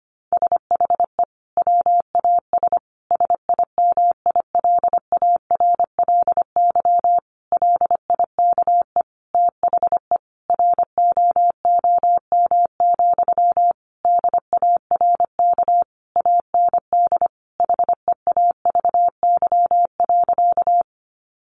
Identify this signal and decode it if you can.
Morse code, heard as a keyed tone, about 25 wpm, 700 Hz. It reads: SHE WAS SIMILARLY LIKE THE ROOM, DARK AND HEAVY.
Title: Sentences from Top 1400 Words 25wpm